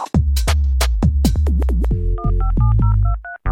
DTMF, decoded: 157023